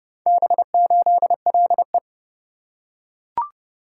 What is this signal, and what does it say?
Morse code, heard as a keyed tone, about 29 wpm, 700 Hz. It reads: B8LE